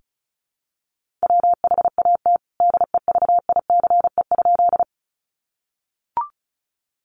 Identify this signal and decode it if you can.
Morse code, heard as a keyed tone, about 34 wpm, 700 Hz. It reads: WHAT DEVICE?
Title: Sentences from Top 1500 Words 35wpm